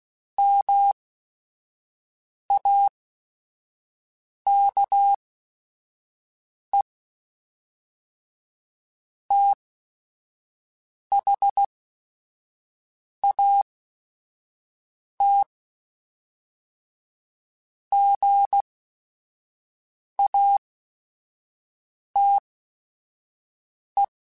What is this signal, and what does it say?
Morse code, heard as a keyed tone, about 16 wpm, spaced as for 5 wpm, 800 Hz. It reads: MAKE THAT GATE
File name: Technical Subjects / Technical Subjects-mp3s / OR Gate @5